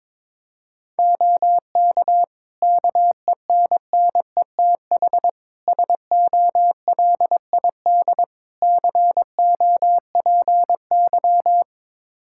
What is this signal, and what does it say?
Morse code, heard as a keyed tone, about 22 wpm, 700 Hz. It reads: OK KENNETH SOLID COPY
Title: Practice QSO - Exchange 3 - More Info - 22wpm